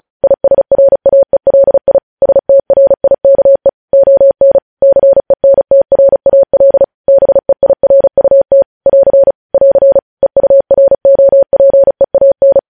Morse code, heard as a keyed tone, about 36 wpm, 550 Hz. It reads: ISRAELI STRIKE ON CENTRAL BEIRUT <AR> <AR> EUROPEAN